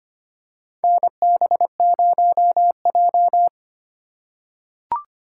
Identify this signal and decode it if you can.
Morse code, heard as a keyed tone, about 25 wpm, 700 Hz. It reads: NB0J
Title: USA Call Signs - 4 Characters - 4x Character Spacing - 25wpm